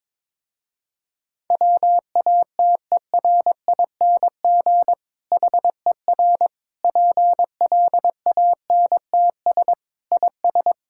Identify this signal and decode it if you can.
Morse code, heard as a keyed tone, about 22 wpm, 700 Hz. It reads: WATERING HER PLANTS IS
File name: Difficult Sentences with Hints - 22wpm